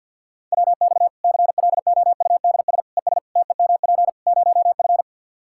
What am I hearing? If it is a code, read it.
Morse code, heard as a keyed tone, about 52 wpm, 700 Hz. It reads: WX CLOUDS ES TEMP 0F
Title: Practice QSO - Exchange 2 - Info - 50wpm